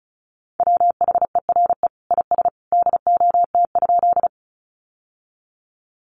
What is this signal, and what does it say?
Morse code, heard as a keyed tone, about 34 wpm, 700 Hz. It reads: WHERE IS DOT?
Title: Sentences from Top 1500 Words 35wpm